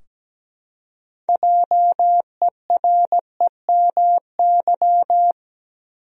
Morse code, heard as a keyed tone, about 17 wpm, 700 Hz. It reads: JEREMY
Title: USA Common Names - 17wpm